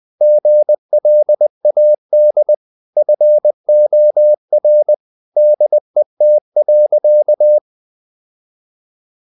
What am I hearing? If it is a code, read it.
Morse code, heard as a keyed tone, about 20 wpm, 600 Hz. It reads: GLAD FOR DET.
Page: Ferie 20 wpm